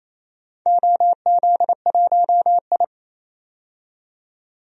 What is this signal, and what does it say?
Morse code, heard as a keyed tone, about 28 wpm, 700 Hz. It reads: OZ1I
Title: DXCC-100 Call Signs - 4 Characters - 28wpm